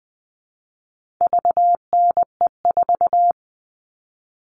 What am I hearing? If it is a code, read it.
Morse code, heard as a keyed tone, about 20 wpm, 700 Hz. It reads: VNE4
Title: All Cut Numbers - 4-Digit 20wpm